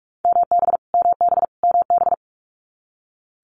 Morse code, heard as a keyed tone, about 46 wpm, 700 Hz. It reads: MB MB MB
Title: Multi-letter Phonemes Repeated 3x 45wpm